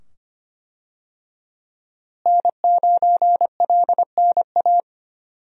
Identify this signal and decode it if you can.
Morse code, heard as a keyed tone, about 25 wpm, 700 Hz. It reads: N9LNA
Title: USA Call Signs - 5 Characters - Repeated 2x - 25wpm